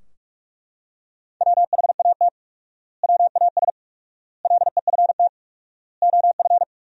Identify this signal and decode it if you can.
Morse code, heard as a keyed tone, about 41 wpm, 700 Hz. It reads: WHAT WAS LEFT OF
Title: 4x Word-Spacing - Sentences from Top 300 Words - 45wpm